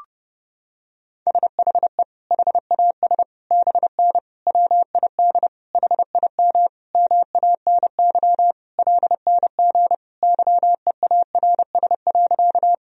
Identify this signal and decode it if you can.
Morse code, heard as a keyed tone, about 30 wpm, 700 Hz. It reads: SHE HAS BN WID HIM MANY LNG YEARS.
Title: Sentences from Top 200 Words and Abbreviations 30wpm